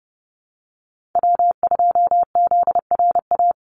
Morse code, heard as a keyed tone, about 29 wpm, 700 Hz. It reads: W2ZRA
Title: Call Signs - 30wpm